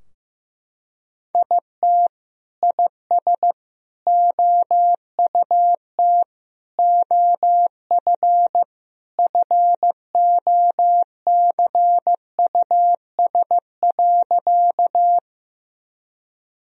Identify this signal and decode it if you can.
Morse code, heard as a keyed tone, about 16 wpm, 700 Hz. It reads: IT IS OUT OF FOCUS.
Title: Sentences from Top 1000 Words - Review Part 4 - 15wpm